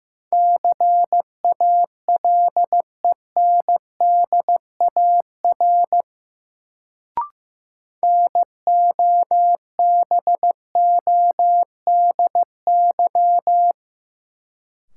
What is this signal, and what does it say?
Morse code, heard as a keyed tone, about 15 wpm, 700 Hz. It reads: CALENDAR NOBODY
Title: Sentences from Top 1600 Words - New Word Drill - 15wpm